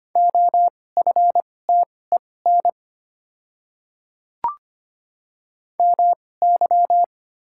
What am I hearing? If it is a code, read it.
Morse code, heard as a keyed tone, about 25 wpm, spaced as for 18 wpm, 700 Hz. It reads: OFTEN MY
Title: 2x Character Spacing - Top 500 Words - 25wpm